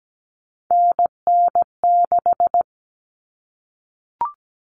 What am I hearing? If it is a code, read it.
Morse code, heard as a keyed tone, about 17 wpm, 700 Hz. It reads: NN6
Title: Common Cut Numbers - 3-Digit 17wpm